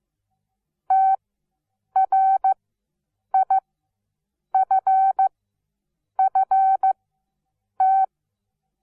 Morse code, heard as a keyed tone, about 15 wpm, spaced as for 7 wpm, 770 Hz. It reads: TRIFFT
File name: Onlinelehrgang / morsen / DJ4UF Morsen17